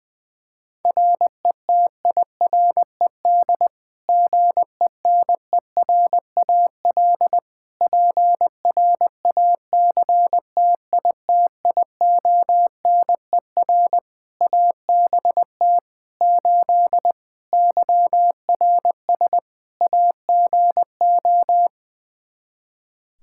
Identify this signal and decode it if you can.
Morse code, heard as a keyed tone, about 20 wpm, 700 Hz. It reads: RETIRED GENERAL PRACTITIONER ABT 8 YRS AGO
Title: Practice QSO - Exchange 3 - More Info - 20wpm